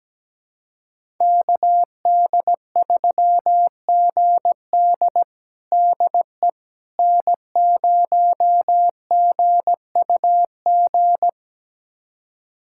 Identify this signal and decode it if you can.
Morse code, heard as a keyed tone, about 17 wpm, 700 Hz. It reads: KD3GD DE N0GUG
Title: USA Callsign DE Callsign - 5 Characters - 17wpm